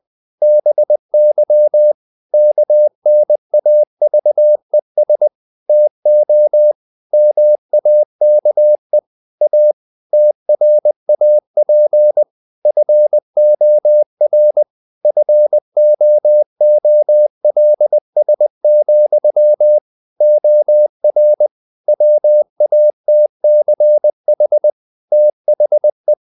Morse code, heard as a keyed tone, about 20 wpm, 600 Hz. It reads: BY KNAVES TO MAKE A TRAP FOR FOOLS, OR WATCH THE
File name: CW / Practice / If20